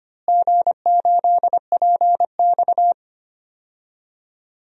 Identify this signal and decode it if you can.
Morse code, heard as a keyed tone, about 25 wpm, 700 Hz. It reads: G8PX
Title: DXCC-100 Call Signs - 4 Characters - Repeated 3x - 25wpm